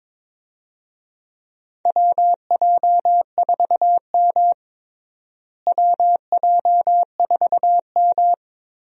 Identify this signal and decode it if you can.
Morse code, heard as a keyed tone, about 22 wpm, 700 Hz. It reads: WJ4M WJ4M
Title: USA Call Signs - 4 Characters - Repeated 3x - 22wpm